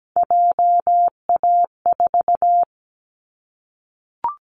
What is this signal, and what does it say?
Morse code, heard as a keyed tone, about 17 wpm, 700 Hz. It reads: JA4
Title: Callsign Prefixes for Top 25 DXCC - Repeated 3x - 17wpm